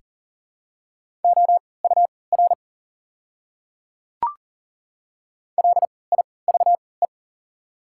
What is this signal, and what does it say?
Morse code, heard as a keyed tone, about 40 wpm, spaced as for 23 wpm, 690 Hz. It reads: OUR LIVE
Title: 3x Character Spacing - Top 200 Words - 40wpm